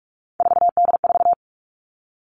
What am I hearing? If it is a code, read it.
Morse code, heard as a keyed tone, about 44 wpm, 700 Hz. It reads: 4D4
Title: All Cut Numbers - 3-Digit 45wpm